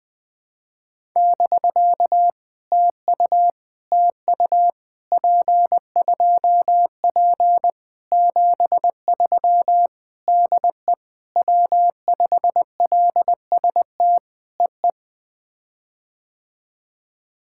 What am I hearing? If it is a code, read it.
Morse code, heard as a keyed tone, about 19 wpm, 700 Hz. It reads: <BK> TU TU P2P 73 DE W5LST EE
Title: POTA QSO Practice - 20wpm